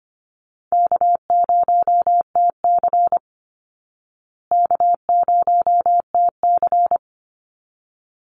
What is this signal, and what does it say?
Morse code, heard as a keyed tone, about 25 wpm, 700 Hz. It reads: K0TC K0TC
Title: Easy Call Signs Repeated 2x - 25wpm